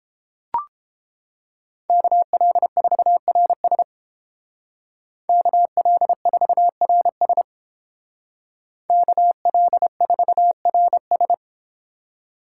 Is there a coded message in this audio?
Yes: Morse code from 1.9 to 11.3 s, heard as a keyed tone, about 29 wpm, 700 Hz. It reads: KL4RS KL4RS KL4RS